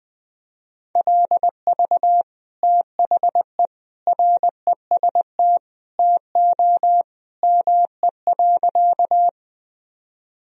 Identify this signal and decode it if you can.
Morse code, heard as a keyed tone, about 21 wpm, 700 Hz. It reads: LV THE REST TO ME.
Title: Sentences from Top 500 Words and Abbreviations 20wpm